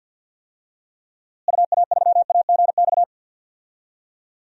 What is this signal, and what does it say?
Morse code, heard as a keyed tone, about 52 wpm, 700 Hz. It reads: UA3AGX
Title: DXCC-100 Call Signs - 6 Characters - Repeated 3x - 50wpm